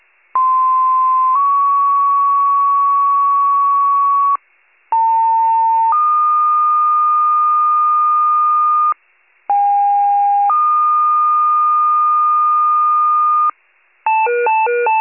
Two-tone page, heard as a one-second tone, then a three-second tone, about 1040 then 1120 Hz, about 900 then 1220 Hz, about 810 then 1220 Hz. It is heard from 0.3 to 13.5 s.